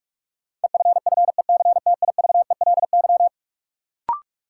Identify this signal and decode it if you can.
Morse code, heard as a keyed tone, about 45 wpm, 700 Hz. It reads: EFFECTIVELY
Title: Sentences from Top 1400 Words - New Word Drill - 45wpm